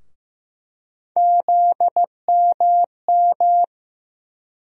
Morse code, heard as a keyed tone, about 15 wpm, 700 Hz. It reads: ZMM
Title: USA Call Sign Postfixes - 3 Letters 15wpm